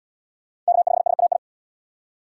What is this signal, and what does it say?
Morse code, heard as a keyed tone, about 72 wpm, 700 Hz. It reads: K5SAI